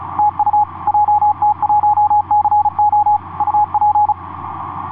Morse code, heard as a keyed tone, about 36 wpm, 840 Hz, with radio noise. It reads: TU JT1CO UP
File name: jt1co dj2da etc 30m cw 01apr2010 1614